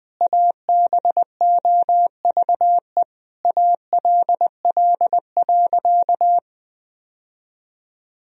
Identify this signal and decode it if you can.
Morse code, heard as a keyed tone, about 20 wpm, 700 Hz. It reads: ABOVE ALL.
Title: Sentences from Top 1000 Words - Review Part 2 - 20wpm